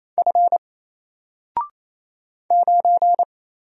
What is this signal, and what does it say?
Morse code, heard as a keyed tone, about 27 wpm, 700 Hz. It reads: F 9
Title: Single Character (Letter:Number:Punctuation:Prosign) 28wpm